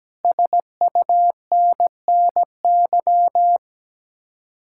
Morse code, heard as a keyed tone, about 17 wpm, 700 Hz. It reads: SUNNY
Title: Speed-Racing Common QSO Phrases Encore - 17wpm